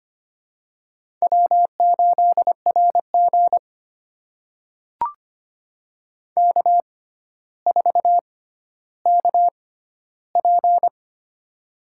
Morse code, heard as a keyed tone, about 25 wpm, spaced as for 13 wpm, 700 Hz. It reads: W8RG K4KP
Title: USA Call Signs - 4 Characters - 6x Character Spacing - 25wpm